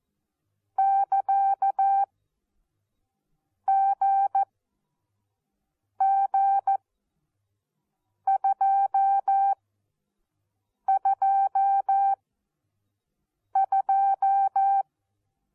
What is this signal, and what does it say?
Morse code, heard as a keyed tone, about 14 wpm, spaced as for 5 wpm, 780 Hz. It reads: <KA> GG222
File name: Onlinelehrgang / morsen / DJ4UF Morsen14